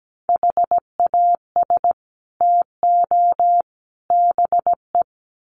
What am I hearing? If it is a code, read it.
Morse code, heard as a keyed tone, about 17 wpm, 700 Hz. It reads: HAS TO BE
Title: Sets of 3 Words 17wpm